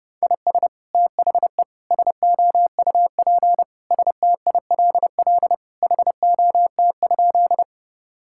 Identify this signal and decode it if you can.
Morse code, heard as a keyed tone, about 30 wpm, 700 Hz. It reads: IS THE SOUP STILL HOT?
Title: Sentences from Top 1000 Words - Review Part 6 - 30wpm